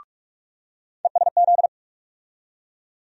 Morse code, heard as a keyed tone, about 45 wpm, 700 Hz. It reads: ESZ
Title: USA Call Sign Postfixes - 3 Letters 45wpm